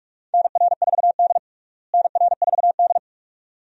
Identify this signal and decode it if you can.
Morse code, heard as a keyed tone, about 46 wpm, 700 Hz. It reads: NR4D NR4D
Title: USA Call Signs - 4 Characters - Repeated 2x - 45wpm